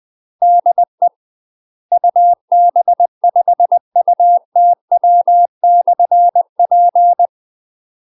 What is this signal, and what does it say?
Morse code, heard as a keyed tone, about 20 wpm, 700 Hz. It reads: DE UB5UTW/P
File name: UK MASTER QSOs TRAINING / 20WPM / UK MASTER QSOs 0020